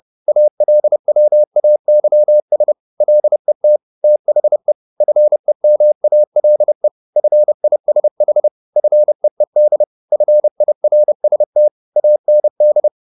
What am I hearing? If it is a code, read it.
Morse code, heard as a keyed tone, about 30 wpm, 600 Hz. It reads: ALWAYS LET THE FEMALE FISH FEED FIRST AND
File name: the old man and the sea / the old man and the sea 1 30wpm